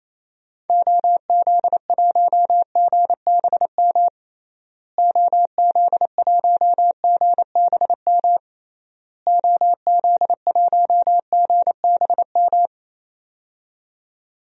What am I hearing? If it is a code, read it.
Morse code, heard as a keyed tone, about 27 wpm, 700 Hz. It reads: OZ1GBM OZ1GBM OZ1GBM
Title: DXCC-100 Call Signs - 6 Characters - Repeated 3x - 28wpm